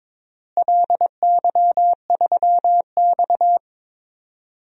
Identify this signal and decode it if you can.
Morse code, heard as a keyed tone, about 22 wpm, 700 Hz. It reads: LY3X
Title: DXCC-100 Call Signs - 4 Characters - Repeated 3x - 22wpm